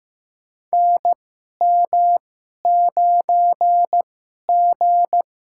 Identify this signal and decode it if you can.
Morse code, heard as a keyed tone, about 15 wpm, spaced as for 11 wpm, 700 Hz. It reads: NM9G
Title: USA Call Signs - 4 Characters - 2x Character Spacing - 15wpm